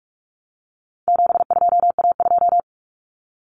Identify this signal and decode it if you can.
Morse code, heard as a keyed tone, about 44 wpm, 700 Hz. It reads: 72A2
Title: Common Cut Numbers - 4-Digit 45wpm